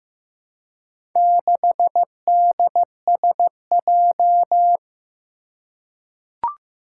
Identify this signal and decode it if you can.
Morse code, heard as a keyed tone, about 15 wpm, 700 Hz. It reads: 6DSJ
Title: Four Challenging Letter-Number Combinations 15wpm